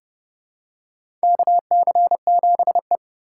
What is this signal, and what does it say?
Morse code, heard as a keyed tone, about 29 wpm, 700 Hz. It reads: KC7E
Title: USA Call Signs - 4 Characters - 4x Character Spacing - 30wpm